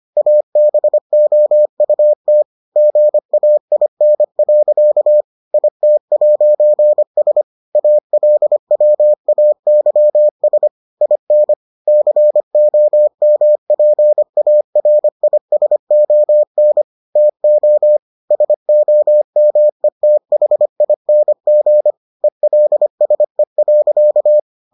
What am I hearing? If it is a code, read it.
Morse code, heard as a keyed tone, about 25 wpm, 600 Hz. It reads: ABOUT GAIN. IT'S ALWAYS IN COMPARISON TO SOMETHING ELSE.